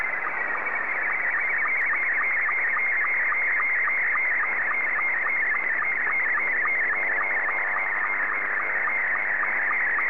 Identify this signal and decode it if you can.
SSTV (slow-scan television), Scottie 2.
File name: sstv / S2-JA3QJA